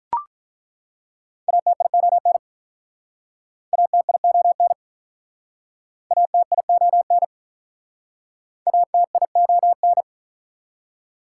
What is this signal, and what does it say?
Morse code, heard as a keyed tone, about 48 wpm, 700 Hz. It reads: ATION ATION ATION ATION